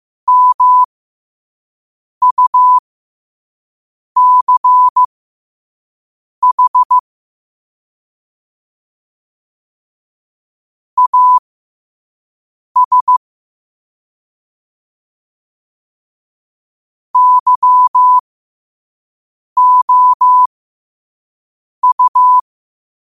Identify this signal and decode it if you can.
Morse code, heard as a keyed tone, about 15 wpm, spaced as for 5 wpm, 1000 Hz. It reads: MUCH AS YOU
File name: qotdfiles / 5wpm